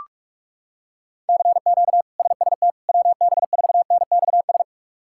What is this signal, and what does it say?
Morse code, heard as a keyed tone, about 46 wpm, 700 Hz. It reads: CQ SST WB4NXS
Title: K1USN SST QSO Practice - 45wpm